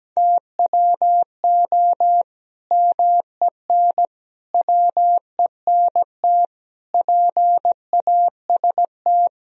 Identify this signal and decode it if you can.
Morse code, heard as a keyed tone, about 17 wpm, 700 Hz. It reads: TWO MEN WENT PAST
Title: Sentences from Top 1000 Words - Review Part 3 - 17wpm